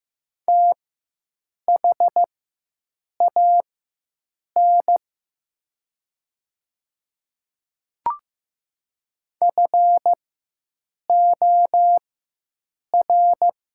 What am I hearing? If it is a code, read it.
Morse code, heard as a keyed tone, about 15 wpm, spaced as for 7 wpm, 700 Hz. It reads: THAN FOR
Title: 4x Character Spacing - Top 100 Words - 15wpm